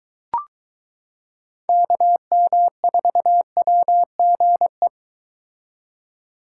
Morse code, heard as a keyed tone, about 22 wpm, 700 Hz. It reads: KM4WGE